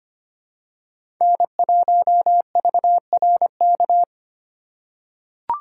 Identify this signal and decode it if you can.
Morse code, heard as a keyed tone, about 25 wpm, 700 Hz. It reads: N1VRK
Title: USA Call Signs - 5 Characters - 3x Character Spacing - 25wpm